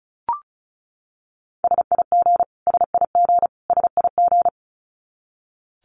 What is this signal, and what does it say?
Morse code, heard as a keyed tone, about 34 wpm, 700 Hz. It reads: SIG SIG SIG
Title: Abbreviations Repeated 3x - 35wpm